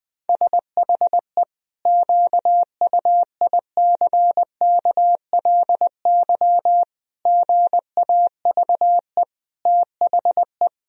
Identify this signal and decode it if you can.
Morse code, heard as a keyed tone, about 20 wpm, 700 Hz. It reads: SHE QUICKLY GAVE THE
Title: Sentences from Top 1000 Words - Review Part 4 - 20wpm